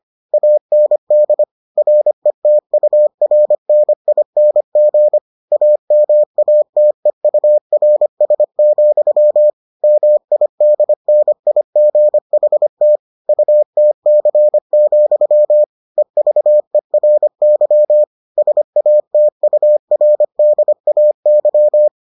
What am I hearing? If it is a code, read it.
Morse code, heard as a keyed tone, about 25 wpm, 600 Hz. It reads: AND RETURNING AMATEURS, MIDNIGHT UTC, EVERY SATURDAY